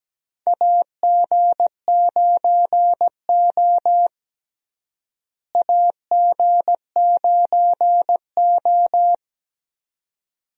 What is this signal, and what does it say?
Morse code, heard as a keyed tone, about 17 wpm, 700 Hz. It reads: AG9O AG9O
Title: USA Call Signs - 4 Characters - Repeated 2x - 17wpm